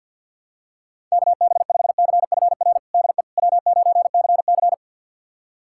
Morse code, heard as a keyed tone, about 51 wpm, 700 Hz. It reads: KB5CFR DE W9CC
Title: USA Callsign DE Callsign 50wpm